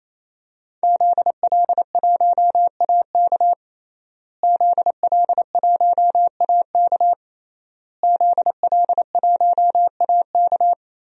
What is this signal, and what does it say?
Morse code, heard as a keyed tone, about 27 wpm, 700 Hz. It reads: ZL1AK ZL1AK ZL1AK
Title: DXCC-100 Call Signs - 5 Characters - Repeated 3x - 28wpm